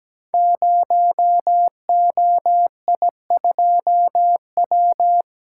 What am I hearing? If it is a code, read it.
Morse code, heard as a keyed tone, about 17 wpm, 700 Hz. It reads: 0OI2W
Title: Code Groups - 5 Letter:Numbers - 17wpm